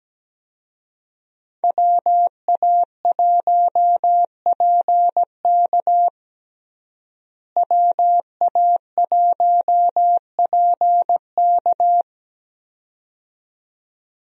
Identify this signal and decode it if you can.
Morse code, heard as a keyed tone, about 17 wpm, 700 Hz. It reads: WA1PK WA1PK
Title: USA Call Signs - 5 Characters - Repeated 2x - 17wpm